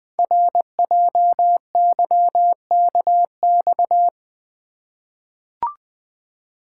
Morse code, heard as a keyed tone, about 20 wpm, 700 Hz. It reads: RJYKX
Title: Code Groups - 5 Letters - 20wpm